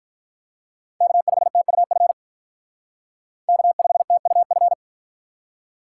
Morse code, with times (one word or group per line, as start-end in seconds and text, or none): 1.0-2.1 K5TUF
3.5-4.7 K5TUF